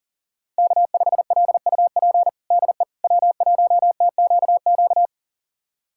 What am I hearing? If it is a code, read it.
Morse code, heard as a keyed tone, about 40 wpm, 700 Hz. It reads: K5LUP DE W1TQQ
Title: USA Callsign DE Callsign - 5 Characters - 40wpm